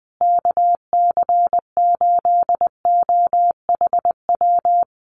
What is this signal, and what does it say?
Morse code, heard as a keyed tone, about 20 wpm, 700 Hz. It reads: KC8OHW
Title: USA Call Signs Repeated 3x 20wpm